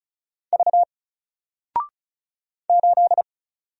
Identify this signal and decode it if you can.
Morse code, heard as a keyed tone, about 36 wpm, 700 Hz. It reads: V 8
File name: Single-Letter-Number-35wpm